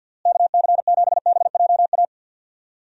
Morse code, heard as a keyed tone, about 50 wpm, 700 Hz. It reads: KC7BJA
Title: USA Call Signs - 6 Characters - Repeated 3x - 50wpm